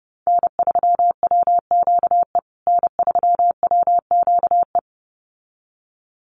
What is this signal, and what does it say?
Morse code, heard as a keyed tone, about 30 wpm, 700 Hz. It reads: N3WQE N3WQE
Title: USA Call Signs Repeated 2x 30wpm